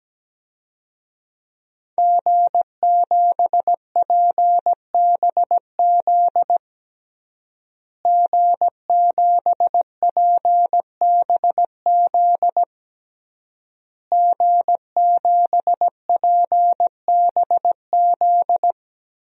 Morse code, heard as a keyed tone, about 17 wpm, 700 Hz. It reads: G7PBZ G7PBZ G7PBZ
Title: DXCC-100 Call Signs - 5 Characters - Repeated 3x - 17wpm